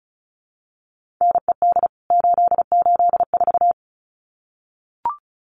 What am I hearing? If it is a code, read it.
Morse code, heard as a keyed tone, about 34 wpm, 700 Hz. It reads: NED 884
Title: CWT Exchange 35wpm